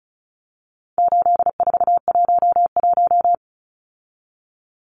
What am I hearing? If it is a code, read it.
Morse code, heard as a keyed tone, about 34 wpm, 700 Hz. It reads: 8411
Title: Numbers - 4-Digit 35wpm